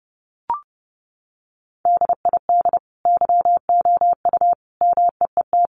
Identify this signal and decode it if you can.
Morse code, heard as a keyed tone, about 30 wpm, 700 Hz. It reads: DID YOU MEET
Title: Sentences from Top 1100 Words 30wpm